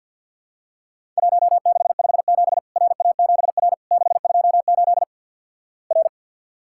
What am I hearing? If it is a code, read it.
Morse code, heard as a keyed tone, about 50 wpm, 700 Hz. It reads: 1657 RA7R 628 R
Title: Worked All Europe DX Contest - QTC Exchange - 50wpm